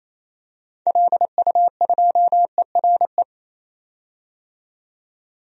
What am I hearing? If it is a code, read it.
Morse code, heard as a keyed tone, about 28 wpm, 700 Hz. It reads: LU2ERE
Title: DXCC-100 Call Signs - 6 Characters - 28wpm